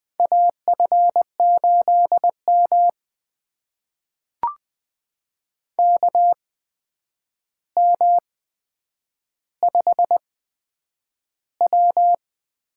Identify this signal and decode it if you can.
Morse code, heard as a keyed tone, about 20 wpm, spaced as for 9 wpm, 700 Hz. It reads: AF8M KM5W
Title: USA Call Signs - 8x Character Spacing - 20wpm